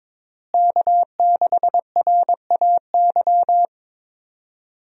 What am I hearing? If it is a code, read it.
Morse code, heard as a keyed tone, about 22 wpm, 700 Hz. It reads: K6RAY
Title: USA Call Signs - 5 Characters - Repeated 2x - 22wpm